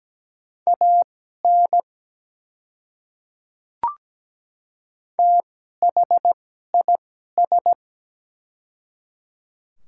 Morse code, heard as a keyed tone, about 17 wpm, spaced as for 12 wpm, 700 Hz. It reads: AN THIS
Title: 2x Character Spacing - Top 300 Words - 17wpm